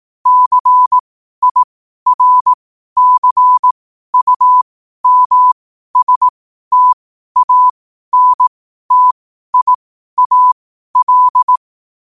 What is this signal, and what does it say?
Morse code, heard as a keyed tone, about 18 wpm, spaced as for 12 wpm, 1000 Hz. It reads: CIRCUMSTANTIAL
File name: qotdfiles / 12wpm